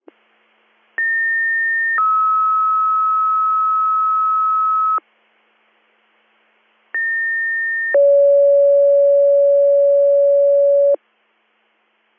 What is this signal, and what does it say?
Two-tone page, heard as a one-second tone, then a three-second tone, about 1800 then 1290 Hz, about 1800 then 560 Hz. It is heard from 1.0 to 11.0 s.